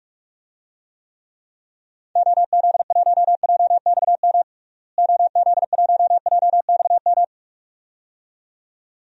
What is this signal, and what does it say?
Morse code, heard as a keyed tone, about 47 wpm, 700 Hz. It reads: OZ1JXM OZ1JXM
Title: DXCC-100 Call Signs - 6 Characters - Repeated 2x - 45wpm